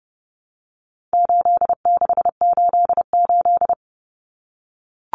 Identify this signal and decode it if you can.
Morse code, heard as a keyed tone, about 30 wpm, 700 Hz. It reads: 8688
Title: Numbers - 4-Digit 30wpm